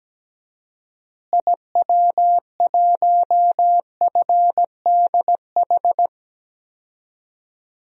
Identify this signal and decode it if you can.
Morse code, heard as a keyed tone, about 17 wpm, 700 Hz. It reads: IW1FDH
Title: DXCC-100 Call Signs - 6 Characters - Repeated 3x - 17wpm